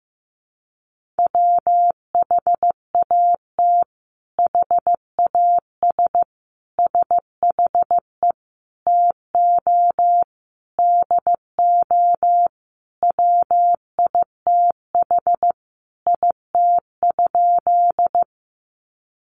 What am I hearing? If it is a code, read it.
Morse code, heard as a keyed tone, about 15 wpm, 700 Hz. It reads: WHAT HAS SHE TO DO WITH IT?
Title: Sentences from Top 100 Words 15wpm